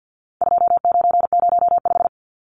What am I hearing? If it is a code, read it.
Morse code, heard as a keyed tone, about 48 wpm, 700 Hz. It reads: 2905
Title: Numbers - 4-Digit 50wpm